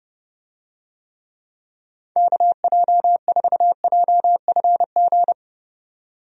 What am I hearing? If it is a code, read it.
Morse code, heard as a keyed tone, about 32 wpm, 700 Hz. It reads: KJ4JFG
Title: USA Call Signs - 6 Characters - 30wpm